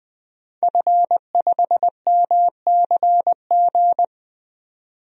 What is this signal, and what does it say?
Morse code, heard as a keyed tone, about 20 wpm, 700 Hz. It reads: F5MCG
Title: DXCC-100 Call Signs - 5 Characters - 20wpm